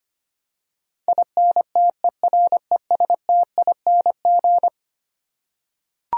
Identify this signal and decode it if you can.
Morse code, heard as a keyed tone, about 25 wpm, 700 Hz. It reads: INTERESTING
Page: morse-code-ninja-practice-25wpm